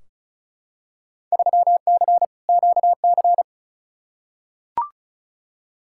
Morse code, heard as a keyed tone, about 36 wpm, 690 Hz. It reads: 3C QC
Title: Field Day Exchanges Encore - 35wpm